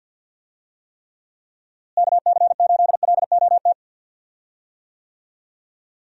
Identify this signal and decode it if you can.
Morse code, heard as a keyed tone, about 52 wpm, 700 Hz. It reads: KC8LOT
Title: USA Call Signs - 6 Characters - Repeated 2x - 50wpm